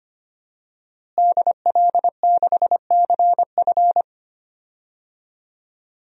Morse code, heard as a keyed tone, about 25 wpm, 700 Hz. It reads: DL6CF
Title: DXCC-100 Call Signs - 5 Characters - Repeated 2x - 25wpm